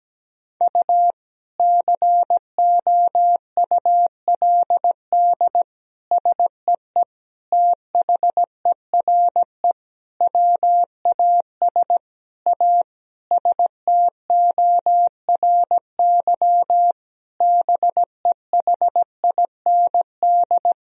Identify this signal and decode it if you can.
Morse code, heard as a keyed tone, about 17 wpm, 700 Hz. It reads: U COULD SEE THERE WAS A STORY BEHIND
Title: Sentences from Top 500 Words and Abbreviations 17wpm